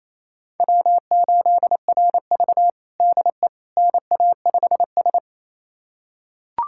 Morse code, heard as a keyed tone, about 28 wpm, 700 Hz. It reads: W8RV DE NA5S
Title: USA Callsign DE Callsign - 4 Characters - 28wpm